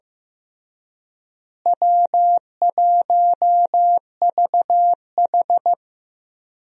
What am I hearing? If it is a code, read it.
Morse code, heard as a keyed tone, about 15 wpm, 700 Hz. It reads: W1VH
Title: USA Call Signs - 4 Characters - Repeated 3x - 15wpm